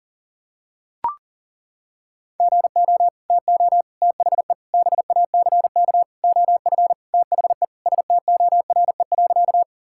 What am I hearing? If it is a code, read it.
Morse code, heard as a keyed tone, about 40 wpm, 700 Hz. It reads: GO TO THE BACK OF THE STORE.
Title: Sentences from Top 1000 Words - Review Part 5 - 40wpm